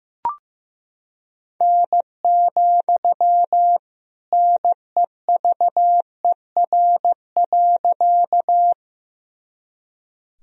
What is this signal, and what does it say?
Morse code, heard as a keyed tone, about 15 wpm, 700 Hz. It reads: N, NEVER.
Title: Sentences from Top 300 Words and Abbreviations 15wpm